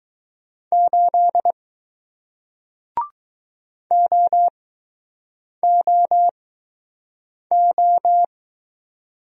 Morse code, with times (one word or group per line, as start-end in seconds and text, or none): 0.7-1.5 8
3.9-8.2 OOO